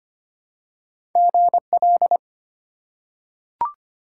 Morse code, heard as a keyed tone, about 25 wpm, 700 Hz. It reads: GL
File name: Top 90 QSO Elements - 2x Character Spacing - 25wpm